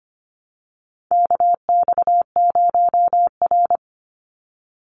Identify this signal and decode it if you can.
Morse code, heard as a keyed tone, about 25 wpm, 700 Hz. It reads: KX0R
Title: NA SOTA Chaser Callsigns 25wpm